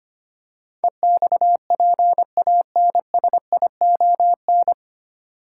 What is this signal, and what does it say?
Morse code, heard as a keyed tone, about 25 wpm, 700 Hz. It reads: EXPANSION
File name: practice / 25wpm / 9 Letter Words - First 3 Letters Given - 25wpm